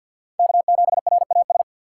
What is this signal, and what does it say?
Morse code, heard as a keyed tone, about 50 wpm, 700 Hz. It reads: K7RAS
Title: USA Call Signs - 5 Characters - 4x Character Spacing - 50wpm